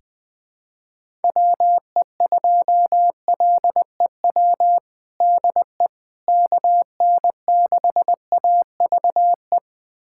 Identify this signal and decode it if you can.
Morse code, heard as a keyed tone, about 20 wpm, 700 Hz. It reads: WE2LEW DE KN6AVE
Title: USA Callsign DE Callsign - 6 Characters - 20wpm